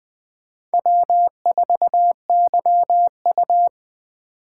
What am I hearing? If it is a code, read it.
Morse code, heard as a keyed tone, about 20 wpm, 700 Hz. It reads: W4YU